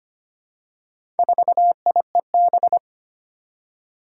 Morse code, heard as a keyed tone, about 25 wpm, 700 Hz. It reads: 4IEB
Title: Four Challenging Letter-Number Combinations 25wpm